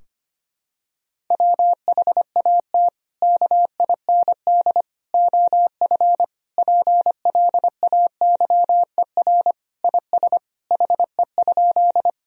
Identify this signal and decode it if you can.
Morse code, heard as a keyed tone, about 25 wpm, 700 Hz. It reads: WHAT KIND OF PLAYER IS HE?
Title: Sentences from Top 1000 Words - Review Part 4 - 25wpm